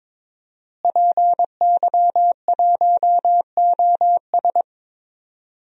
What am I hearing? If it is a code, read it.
Morse code, heard as a keyed tone, about 21 wpm, 700 Hz. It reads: PY1OS
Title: DXCC-100 Call Signs - 5 Characters - Repeated 3x - 22wpm